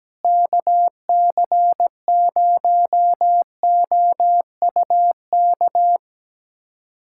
Morse code, heard as a keyed tone, about 17 wpm, 700 Hz. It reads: KC0OUK
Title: USA Callsign DE Callsign - 6 Characters - 17wpm